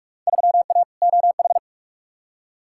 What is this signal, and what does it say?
Morse code, heard as a keyed tone, about 43 wpm, 700 Hz. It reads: 3A OH
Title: Field Day Exchanges - 45wpm